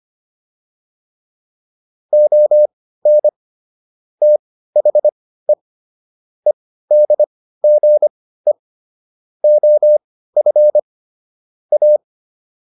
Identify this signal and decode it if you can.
Morse code, heard as a keyed tone, about 25 wpm, spaced as for 15 wpm, 600 Hz. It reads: ON THE EDGE OF A
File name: WD404 15